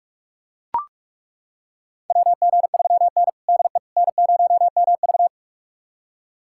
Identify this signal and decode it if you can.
Morse code, heard as a keyed tone, about 41 wpm, 700 Hz. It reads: WG3N DE N0MV
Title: USA Callsign DE Callsign - 4 Characters - 45wpm